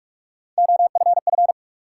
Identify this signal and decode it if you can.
Morse code, heard as a keyed tone, about 46 wpm, 700 Hz. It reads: OFF
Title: Sentences from Top 200 Words - New Word Drill - 45wpm